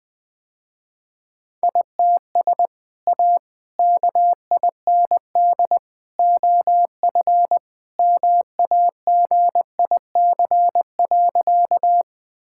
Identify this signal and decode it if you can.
Morse code, heard as a keyed tone, about 21 wpm, 700 Hz. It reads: ITS A KIND OF MAGIC.
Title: Sentences with 3-Word Phrases - Spoken Beginning - 20wpm